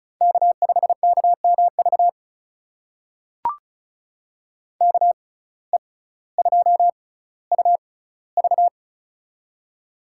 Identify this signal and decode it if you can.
Morse code, heard as a keyed tone, about 36 wpm, spaced as for 18 wpm, 700 Hz. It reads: K5KMV KE2UV
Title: USA Call Signs - 5 Characters - 6x Character Spacing - 35wpm